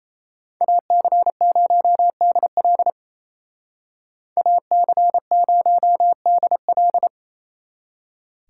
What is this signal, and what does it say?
Morse code, heard as a keyed tone, about 33 wpm, 700 Hz. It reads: AC0DL AC0DL
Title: Speed Racing - USA Call Signs 28wpm